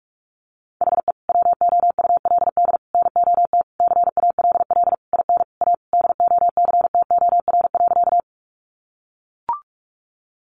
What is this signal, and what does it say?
Morse code, heard as a keyed tone, about 44 wpm, 700 Hz. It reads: HE WOULD NOT CALL IN A DOCTOR.
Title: Sentences from Top 1500 Words 45wpm